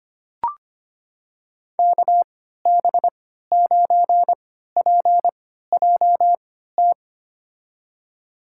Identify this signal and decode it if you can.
Morse code, heard as a keyed tone, about 25 wpm, spaced as for 14 wpm, 700 Hz. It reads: KB9PJT
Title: USA Call Signs - 6 Characters - 3x Character Spacing - 25wpm